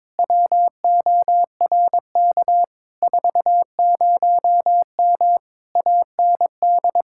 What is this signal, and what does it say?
Morse code, heard as a keyed tone, about 22 wpm, 700 Hz. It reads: WORK 40M AND
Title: Speed-Racing Common QSO Phrases Encore - 22wpm